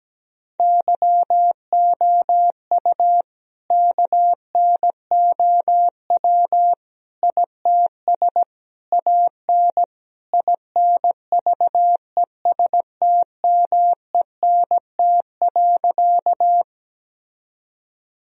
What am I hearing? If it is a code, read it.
Morse code, heard as a keyed tone, about 17 wpm, 700 Hz. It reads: YOU KNOW ITS AN INVESTMENT.
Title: Sentences from Top 1000 Words - Review Part 4 - 17wpm